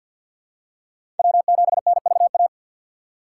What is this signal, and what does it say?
Morse code, heard as a keyed tone, about 48 wpm, 690 Hz. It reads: W7NVA
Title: USA Call Signs - 5 Characters - 4x Character Spacing - 50wpm